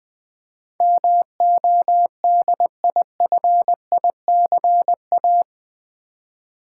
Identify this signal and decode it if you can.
Morse code, heard as a keyed tone, about 21 wpm, 700 Hz. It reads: MODIFICA
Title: Build A Word - 4 Letters At A Time - 12-Letter Words - 20wpm